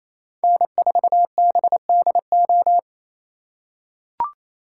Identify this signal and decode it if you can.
Morse code, heard as a keyed tone, about 28 wpm, 700 Hz. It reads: N4BDO
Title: USA Call Signs - 5 Characters - 2x Character Spacing - 28wpm